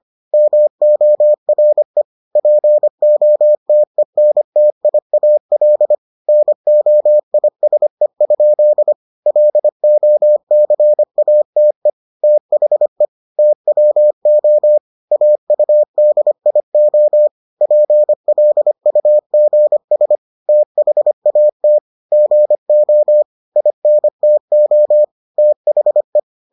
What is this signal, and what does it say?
Morse code, heard as a keyed tone, about 25 wpm, 600 Hz. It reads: MORE POTENTIAL NOISE? LOCATE THE TWO AUDIO PLUGS THAT GO INTO THE